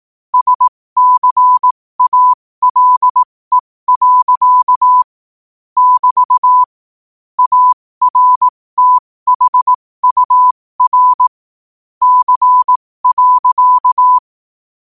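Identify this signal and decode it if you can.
Morse code, heard as a keyed tone, about 18 wpm, spaced as for 15 wpm, 1000 Hz. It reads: SCALE. <BT> ARTHUR C.
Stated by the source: Morse quote-of-the-day practice at 15 WPM from Morse Resource.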